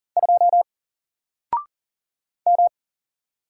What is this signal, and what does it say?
Morse code, heard as a keyed tone, about 40 wpm, 700 Hz. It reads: 2 M
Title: Single Letter-Number 40wpm